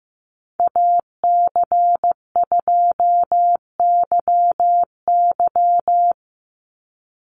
Morse code, heard as a keyed tone, about 15 wpm, 700 Hz. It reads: AC2YY
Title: USA Call Signs 15wpm